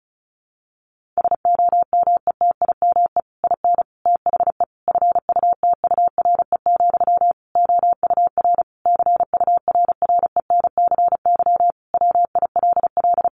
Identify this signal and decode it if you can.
Morse code, heard as a keyed tone, about 34 wpm, 700 Hz. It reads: SOMETIME IN THE FUTURE, OUR CURRENCY WILL
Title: Sentences from Top 1500 Words 35wpm